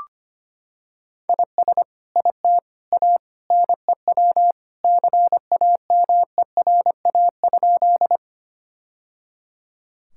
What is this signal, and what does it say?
Morse code, heard as a keyed tone, about 25 wpm, 700 Hz. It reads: IS IT A NEW CAMERA?
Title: Sentences from Top 1000 Words - Review Part 5 - 25wpm